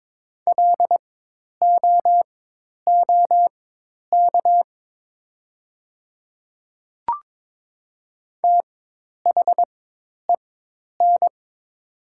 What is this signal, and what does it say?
Morse code, heard as a keyed tone, about 22 wpm, spaced as for 10 wpm, 700 Hz. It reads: LOOK THEN
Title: 4x Character Spacing - Top 100 Words - 22wpm